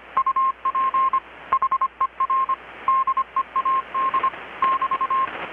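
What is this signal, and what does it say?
Morse code, heard as a keyed tone, about 25 wpm, 1060 Hz. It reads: UP HER DEAD <BT>
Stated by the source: Morse from KPH's Night of Nights XX maritime radiotelegraphy event, received in Chiapas, Mexico about 3,800 km away on a Kenwood R-600 with a grounded 10 m long-wire antenna.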